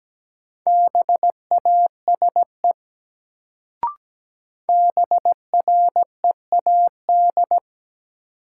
Morse code, heard as a keyed tone, about 17 wpm, 700 Hz. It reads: BASE BREAD
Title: Sentences from Top 1600 Words - New Word Drill Alt - 17wpm